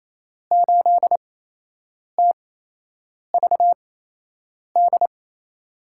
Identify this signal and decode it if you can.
Morse code, heard as a keyed tone, about 28 wpm, spaced as for 8 wpm, 700 Hz. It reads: 8TVD
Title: USA Call Signs - 6 Characters - 8x Character Spacing - 28wpm